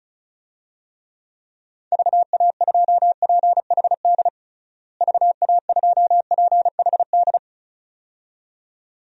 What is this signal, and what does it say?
Morse code, heard as a keyed tone, about 34 wpm, 700 Hz. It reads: VA2PHD VA2PHD
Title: DXCC-100 Call Signs - 6 Characters - Repeated 2x - 35wpm